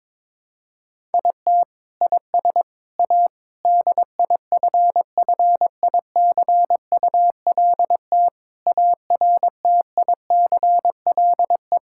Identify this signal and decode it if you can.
Morse code, heard as a keyed tone, about 22 wpm, 700 Hz. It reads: IT IS A DIFFICULT ARTICLE
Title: Sentences from Top 1000 Words - Review Part 5 - 22wpm